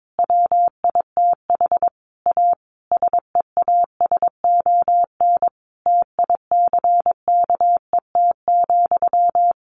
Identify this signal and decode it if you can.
Morse code, heard as a keyed tone, about 22 wpm, 700 Hz. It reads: WITH A SEASON TICKET,
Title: Sentences from Top 1400 Words 22wpm